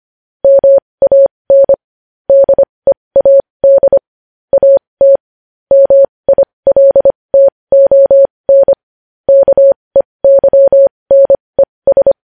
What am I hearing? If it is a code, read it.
Morse code, heard as a keyed tone, about 25 wpm, spaced as for 20 wpm, 550 Hz. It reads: MAN DEAD AT MILTON KEYNES
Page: news-headlines-in-morse-code-at-20-wpm